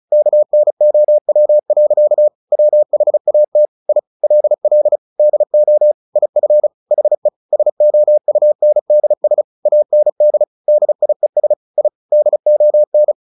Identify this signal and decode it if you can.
Morse code, heard as a keyed tone, about 34 wpm, 600 Hz. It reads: KNOW. WHAT I LL DO IF HE SOUNDS AND DIES I DON
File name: the old man and the sea / the old man and the sea 1 35wpm